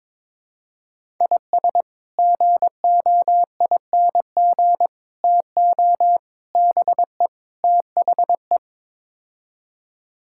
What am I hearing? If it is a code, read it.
Morse code, heard as a keyed tone, about 21 wpm, 700 Hz. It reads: IS GOING TO BE THE
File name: practice / 22wpm / Build A Phrase - 5-Word Phrases - 22wpm